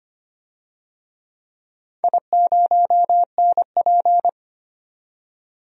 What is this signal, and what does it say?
Morse code, heard as a keyed tone, about 25 wpm, 700 Hz. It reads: I0NP
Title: DXCC-100 Call Signs - 4 Characters - 25wpm